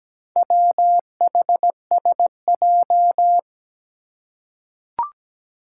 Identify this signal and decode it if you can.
Morse code, heard as a keyed tone, about 17 wpm, 700 Hz. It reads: WHSJ
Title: Four Challenging Letter-Number Combinations 17wpm